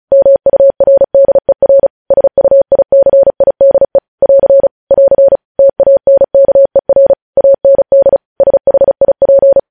Morse code, heard as a keyed tone, about 36 wpm, 550 Hz. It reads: MURDER SUICIDE <AR> <AR> TANKER AND SHIP
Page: news headlines in morse code at 35 wpm